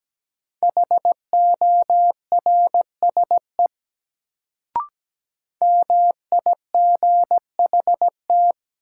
Morse code, heard as a keyed tone, about 17 wpm, 700 Hz. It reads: HORSE MIGHT
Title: Sentences from Top 300 Words - New Word Drill - 17wpm